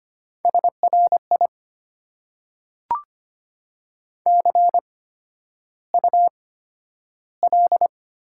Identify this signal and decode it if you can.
Morse code, heard as a keyed tone, about 25 wpm, spaced as for 11 wpm, 700 Hz. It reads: SRI CUL
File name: Top 90 QSO Elements - 8x Character Spacing - 25wpm